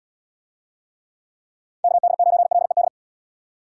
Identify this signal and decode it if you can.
Morse code, heard as a keyed tone, about 78 wpm, 700 Hz. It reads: KD9LED